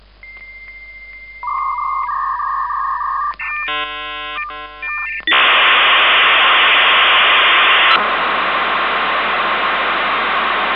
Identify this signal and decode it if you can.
Dial-up modem handshake.